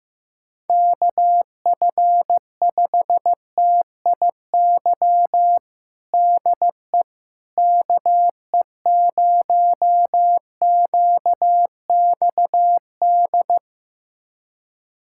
Morse code, heard as a keyed tone, about 16 wpm, 700 Hz. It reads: KF5TIY DE KE0QXD
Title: USA Callsign DE Callsign - 6 Characters - 15wpm